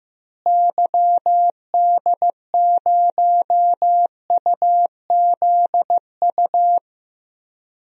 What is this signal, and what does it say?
Morse code, heard as a keyed tone, about 15 wpm, 700 Hz. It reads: YD0UZU
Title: DXCC-100 Call Signs - 6 Characters - Repeated 3x - 15wpm